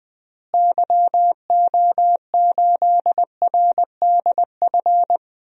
Morse code, heard as a keyed tone, about 20 wpm, 700 Hz. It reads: YO8RDF
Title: DXCC-100 Call Signs - 6 Characters - Repeated 2x - 20wpm